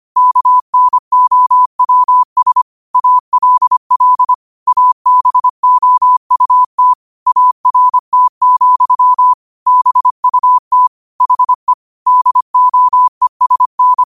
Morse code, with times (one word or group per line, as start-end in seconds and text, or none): 0.2-2.6 KNOWS
2.9-4.3 ALL
4.7-6.9 ABOUT
7.3-9.3 ART,
9.7-10.9 BUT
11.2-11.7 HE
12.1-14.0 DOESN